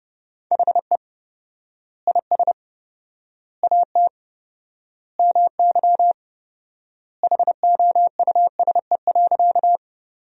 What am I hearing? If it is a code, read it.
Morse code, heard as a keyed tone, about 28 wpm, 700 Hz. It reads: HE IS AT MY HOUSE.
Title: 4x Word-Spacing - Sentences from Top 200 Words - 30wpm